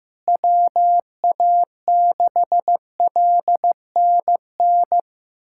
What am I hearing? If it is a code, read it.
Morse code, heard as a keyed tone, about 15 wpm, 700 Hz. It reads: WA6LNN
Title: USA Callsign DE Callsign 15wpm